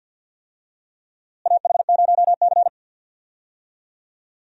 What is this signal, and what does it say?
Morse code, heard as a keyed tone, about 46 wpm, 700 Hz. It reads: AH0C
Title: USA Call Signs - 4 Characters - Repeated 2x - 50wpm